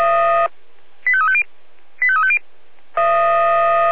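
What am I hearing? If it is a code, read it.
Five-tone selective call (ZVEI), heard as quick runs of five tones, tones of about 70 ms. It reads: 85350 85350
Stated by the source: ZVEI five-tone selective call received on SDR.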